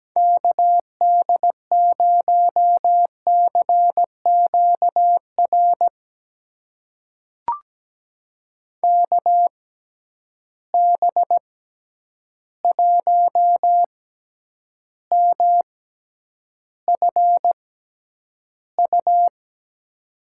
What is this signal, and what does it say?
Morse code, heard as a keyed tone, about 17 wpm, spaced as for 9 wpm, 700 Hz. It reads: KD0CQR KB1MFU
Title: USA Call Signs - 6 Characters - 6x Character Spacing - 17wpm